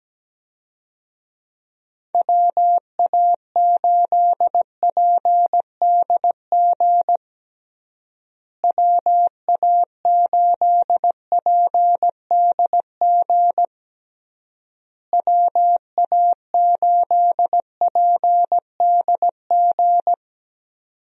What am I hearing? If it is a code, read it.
Morse code, heard as a keyed tone, about 17 wpm, 700 Hz. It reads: WA8PDG WA8PDG WA8PDG
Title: USA Call Signs - 6 Characters - Repeated 3x - 17wpm